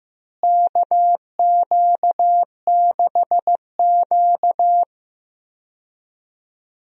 Morse code, heard as a keyed tone, about 15 wpm, 700 Hz. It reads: KQ6Q
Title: USA Call Signs - 4 Characters - Repeated 3x - 15wpm